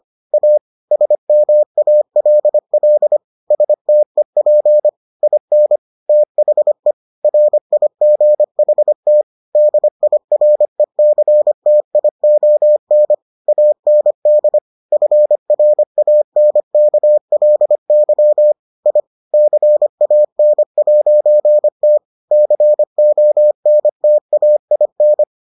Morse code, heard as a keyed tone, about 25 wpm, 600 Hz. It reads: A SMALL STEP IN THE RIGHT DIRECTION AND FRANKLY I CAN'T CONTAIN